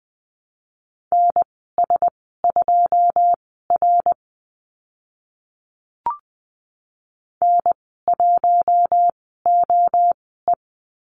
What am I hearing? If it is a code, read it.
Morse code, heard as a keyed tone, about 20 wpm, spaced as for 15 wpm, 700 Hz. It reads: NS2R N1OE
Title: 082 Lesson 14g - Number 2 - Call Signs 20wpm